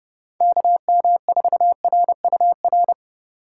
Morse code, heard as a keyed tone, about 31 wpm, 700 Hz. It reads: KM4RUR
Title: USA Call Signs - 6 Characters - 4x Character Spacing - 30wpm